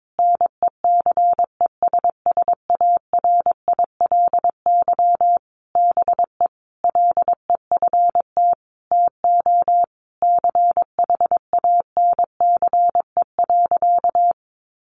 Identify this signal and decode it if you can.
Morse code, heard as a keyed tone, about 22 wpm, 700 Hz. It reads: NECESSARILY BE LEFT TO CHANCE.
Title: Sentences from Top 1400 Words 22wpm